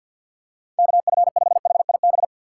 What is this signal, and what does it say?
Morse code, heard as a keyed tone, about 51 wpm, 700 Hz. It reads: KF5HIB
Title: USA Call Signs - 6 Characters - 4x Character Spacing - 50wpm